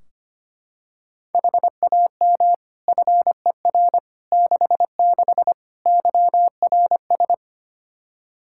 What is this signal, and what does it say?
Morse code, heard as a keyed tone, about 25 wpm, 700 Hz. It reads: HAM FER 66 YRS
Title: Practice QSO - Exchange 3 - More Info - 25wpm